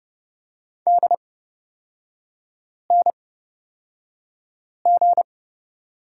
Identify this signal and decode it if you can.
Morse code, heard as a keyed tone, about 31 wpm, spaced as for 5 wpm, 690 Hz. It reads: DNG
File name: Lesson 38e - Letter q - Three Characters 30wpm